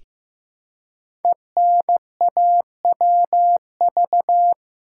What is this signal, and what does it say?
Morse code, heard as a keyed tone, about 15 wpm, 700 Hz. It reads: ENAWV
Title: Code Groups - 5 Letters - 15wpm